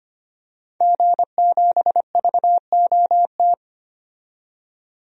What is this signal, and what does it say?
Morse code, heard as a keyed tone, about 25 wpm, 700 Hz. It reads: G7VOT
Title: DXCC-100 Call Signs - 5 Characters - 25wpm